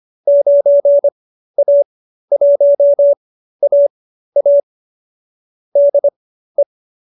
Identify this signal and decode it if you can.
Morse code, heard as a keyed tone, about 25 wpm, spaced as for 13 wpm, 590 Hz. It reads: 9A1AA DE